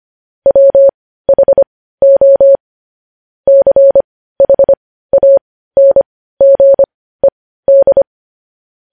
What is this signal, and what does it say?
Morse code, heard as a keyed tone, about 25 wpm, spaced as for 15 wpm, 550 Hz. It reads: WHO CHANGED
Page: News Headlines in Morse Code at 15 WPM